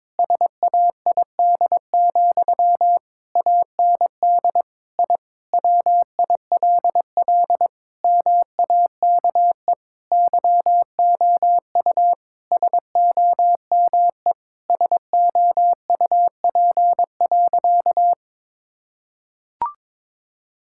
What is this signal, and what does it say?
Morse code, heard as a keyed tone, about 22 wpm, 700 Hz. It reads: SAID, AND I WILL MAKE YOU SOME SOUP.
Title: Sentences from Top 1000 Words - Review Part 6 - 22wpm